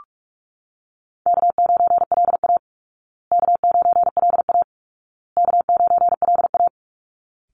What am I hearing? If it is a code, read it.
Morse code, heard as a keyed tone, about 44 wpm, 700 Hz. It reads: K9LA K9LA K9LA
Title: Easy Call Signs Repeated 3x - 45wpm